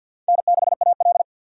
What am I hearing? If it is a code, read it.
Morse code, heard as a keyed tone, about 53 wpm, 700 Hz. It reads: N6AL
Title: USA Call Signs - 4 Characters - 2x Character Spacing - 50wpm